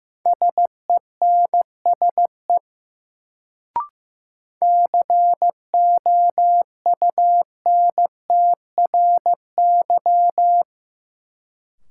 Morse code, heard as a keyed tone, about 15 wpm, 700 Hz. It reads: SENSE COUNTRY
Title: Sets of 2 Words - New Word Drill - 15wpm